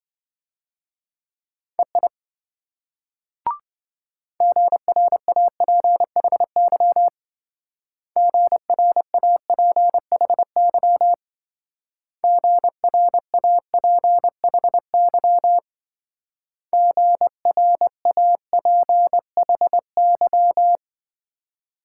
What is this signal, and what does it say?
Morse code, heard as a keyed tone, about 26 wpm, 700 Hz. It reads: EI GRAPHY GRAPHY GRAPHY GRAPHY